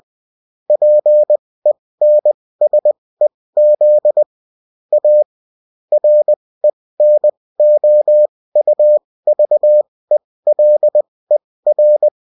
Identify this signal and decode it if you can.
Morse code, heard as a keyed tone, about 20 wpm, spaced as for 16 wpm, 600 Hz. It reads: PENSEZ A RENOUVELER